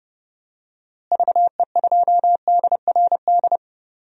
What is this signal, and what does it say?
Morse code, heard as a keyed tone, about 30 wpm, 700 Hz. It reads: VE2DRD
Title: DXCC-100 Call Signs - 6 Characters - Repeated 3x - 30wpm